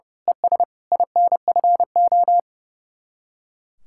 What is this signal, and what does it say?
Morse code, heard as a keyed tone, about 28 wpm, 700 Hz. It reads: ES INFO
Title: Speed-Racing Common QSO Phrases - 30wpm